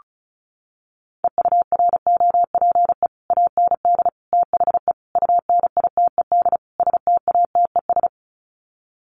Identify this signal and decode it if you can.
Morse code, heard as a keyed tone, about 34 wpm, 700 Hz. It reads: EUROPE AND THE UNITED STATES
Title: Sets of 5 Words Encore2 35wpm